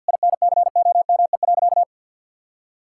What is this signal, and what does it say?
Morse code, heard as a keyed tone, about 50 wpm, 700 Hz. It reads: INCOME.
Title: Sentences from Top 1000 Words - Review Part 5 - 50wpm